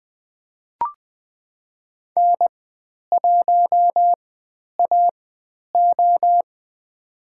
Morse code, heard as a keyed tone, about 20 wpm, spaced as for 10 wpm, 700 Hz. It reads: N1AO
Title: 102 Lesson 17g - Letter u - Call Signs 20wpm